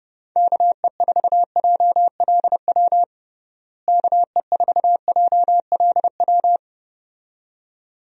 Morse code, heard as a keyed tone, about 30 wpm, 700 Hz. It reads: KE4JLW KE4JLW
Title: USA Call Signs - 6 Characters - Repeated 2x - 30wpm